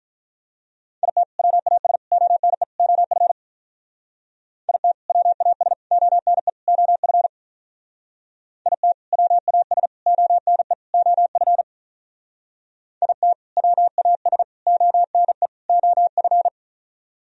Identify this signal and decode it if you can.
Morse code, heard as a keyed tone, about 44 wpm, 700 Hz. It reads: IT WAS ONE OF IT WAS ONE OF IT WAS ONE OF IT WAS ONE OF